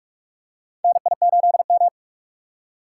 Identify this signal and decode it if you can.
Morse code, heard as a keyed tone, about 44 wpm, 700 Hz. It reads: NI8M
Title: USA Call Signs - 4 Characters - 2x Character Spacing - 45wpm